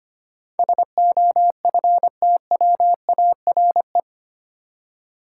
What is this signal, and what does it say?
Morse code, heard as a keyed tone, about 25 wpm, 690 Hz. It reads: SOFTWARE
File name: 8 Letter Words - First 3 Letters Given - 25wpm